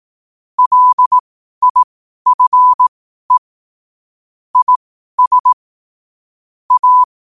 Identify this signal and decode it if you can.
Morse code, heard as a keyed tone, about 18 wpm, spaced as for 12 wpm, 1000 Hz. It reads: LIFE IS A
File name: qotdfiles / 12wpm